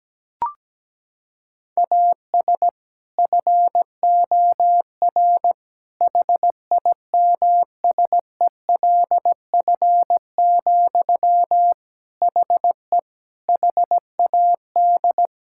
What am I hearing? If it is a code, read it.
Morse code, heard as a keyed tone, about 17 wpm, 700 Hz. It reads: AS FOR HIMSELF, HE HAD
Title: Sentences from Top 1600 Words 17wpm